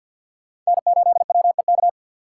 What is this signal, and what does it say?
Morse code, heard as a keyed tone, about 52 wpm, 700 Hz. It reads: N8WEK
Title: USA Call Signs - 3x Character Spacing - 50wpm